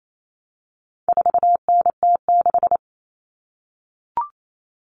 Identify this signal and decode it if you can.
Morse code, heard as a keyed tone, about 28 wpm, 700 Hz. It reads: 4NT6
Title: All Cut Numbers - 4-Digit 28wpm